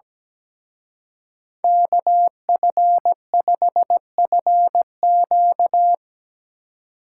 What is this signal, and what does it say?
Morse code, heard as a keyed tone, about 19 wpm, 700 Hz. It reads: KF5FQ
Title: Speed Racing - USA 2x2 Call Signs 17wpm